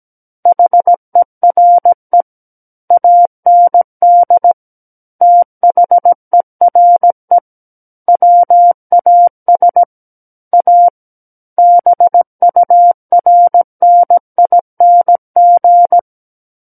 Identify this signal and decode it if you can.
Morse code, heard as a keyed tone, about 17 wpm, 700 Hz. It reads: HERE AND THERE WAS A BURNING
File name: Book One / 08-Chapter 8 - 17 WPM